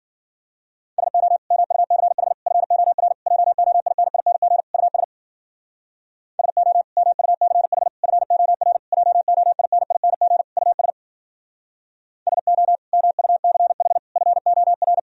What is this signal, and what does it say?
Morse code, heard as a keyed tone, about 52 wpm, 700 Hz. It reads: SO MUCH FOR JOINING US SO MUCH FOR JOINING US SO MUCH FOR